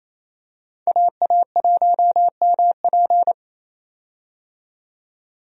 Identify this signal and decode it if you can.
Morse code, heard as a keyed tone, about 27 wpm, 700 Hz. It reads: AA1MP
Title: USA Call Signs - 5 Characters - 28wpm